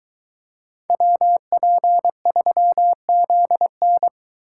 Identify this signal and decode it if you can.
Morse code, heard as a keyed tone, about 23 wpm, 700 Hz. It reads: WP3ZN